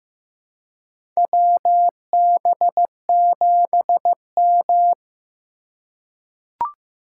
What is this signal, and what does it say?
Morse code, heard as a keyed tone, about 16 wpm, 700 Hz. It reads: WB7M
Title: USA Call Signs - 3x Character Spacing - 15wpm